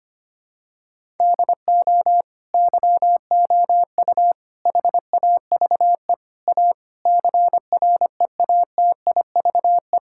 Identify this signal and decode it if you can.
Morse code, heard as a keyed tone, about 25 wpm, 700 Hz. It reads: DO YOU HAVE A CREATIVE
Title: Sentences from Top 1000 Words - Review Part 5 - 25wpm